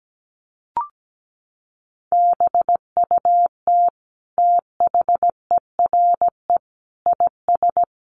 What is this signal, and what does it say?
Morse code, heard as a keyed tone, about 17 wpm, 700 Hz. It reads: BUT THERE IS
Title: Sentences from Top 1500 Words 17wpm